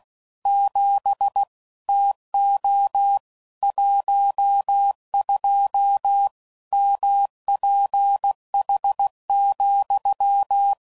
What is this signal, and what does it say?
Morse code, heard as a keyed tone, about 16 wpm, 800 Hz. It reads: 7 TO 12 MPH,
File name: CW Code Groups @15WPM / Weather Sets / Weather 5 22